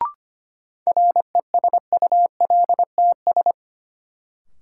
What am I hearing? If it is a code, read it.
Morse code, heard as a keyed tone, about 25 wpm, 700 Hz. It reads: RESULTS
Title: Sets of 4 Words - New Word Drill - 25wpm